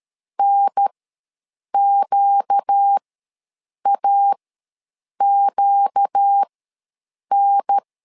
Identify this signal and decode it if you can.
Morse code, heard as a keyed tone, about 13 wpm, spaced as for 7 wpm, 800 Hz. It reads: NQAQN
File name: LEZIONI / 4 Q ws 12wpm 1912